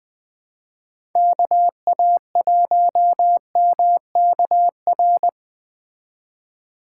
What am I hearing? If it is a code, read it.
Morse code, heard as a keyed tone, about 20 wpm, 700 Hz. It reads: KA1MKR
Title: Speed Racing - USA 2x3 Call Signs 17wpm